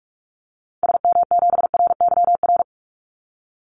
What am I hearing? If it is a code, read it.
Morse code, heard as a keyed tone, about 44 wpm, 700 Hz. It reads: SM7RYR
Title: Call Signs Repeated 2x - 45wpm